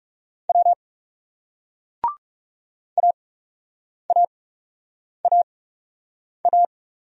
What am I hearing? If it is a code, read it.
Morse code, heard as a keyed tone, about 41 wpm, spaced as for 8 wpm, 700 Hz. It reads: W AAAA